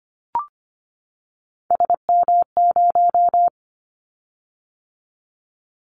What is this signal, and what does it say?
Morse code, heard as a keyed tone, about 25 wpm, 700 Hz. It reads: SM0
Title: Callsign Prefixes for Top 25 DXCC - 25wpm